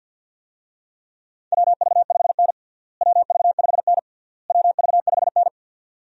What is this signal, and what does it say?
Morse code, heard as a keyed tone, about 52 wpm, 700 Hz. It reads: WV5N WV5N WV5N
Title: USA Call Signs - 4 Characters - Repeated 3x - 50wpm